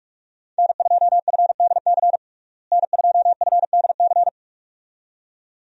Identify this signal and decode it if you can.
Morse code, heard as a keyed tone, about 44 wpm, 700 Hz. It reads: N2FDC N2FDC
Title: USA Call Signs - 5 Characters - Repeated 2x - 45wpm